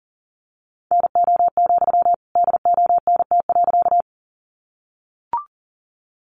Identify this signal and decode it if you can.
Morse code, heard as a keyed tone, about 40 wpm, 700 Hz. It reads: NO, DONT.
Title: Sentences from Top 300 Words 40wpm